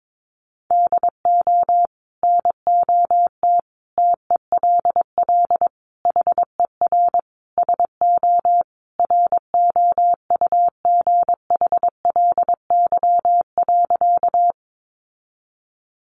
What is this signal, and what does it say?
Morse code, heard as a keyed tone, about 22 wpm, 700 Hz. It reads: DO NOT TELL HER SO ROUGHLY.
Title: Sentences from Top 1500 Words 22wpm